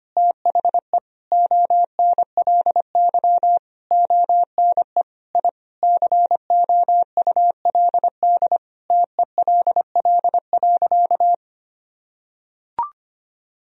Morse code, heard as a keyed tone, about 25 wpm, 700 Hz. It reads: THE ONLY ONE I COULD TELL.
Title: Sentences from Top 1000 Words - Review Part 4 - 25wpm